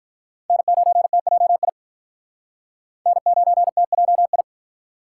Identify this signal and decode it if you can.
Morse code, heard as a keyed tone, about 45 wpm, 700 Hz. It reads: N9TJI N9TJI